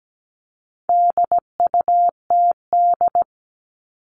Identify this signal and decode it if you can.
Morse code, heard as a keyed tone, about 17 wpm, 700 Hz. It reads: DUTD
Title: All Cut Numbers - 4-Digit 17wpm